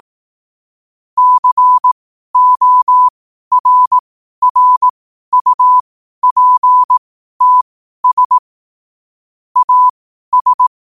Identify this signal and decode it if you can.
Morse code, heard as a keyed tone, about 18 wpm, spaced as for 12 wpm, 1000 Hz. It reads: CORRUPTS AS
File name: qotdfiles / 12wpm